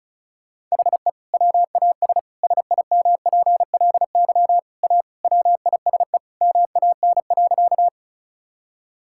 Morse code, heard as a keyed tone, about 34 wpm, 700 Hz. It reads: HE WAS SIMPLY A WISE MAN.
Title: Sentences from Top 1600 Words 35wpm